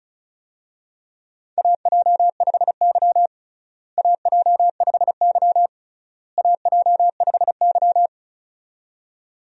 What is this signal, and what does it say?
Morse code, heard as a keyed tone, about 36 wpm, 700 Hz. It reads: AJ5Y AJ5Y AJ5Y
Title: USA Call Signs - 4 Characters - Repeated 3x - 35wpm